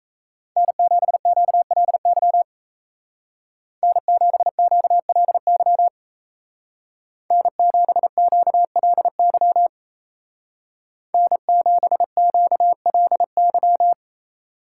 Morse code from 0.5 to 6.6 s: N7QLY N7QLY

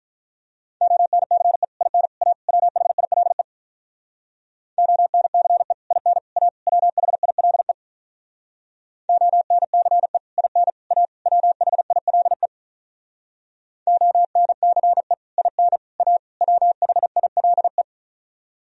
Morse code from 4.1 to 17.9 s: ONCE IN A WHILE ONCE IN A WHILE ONCE IN A WHILE

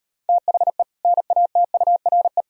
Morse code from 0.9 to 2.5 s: NATURE

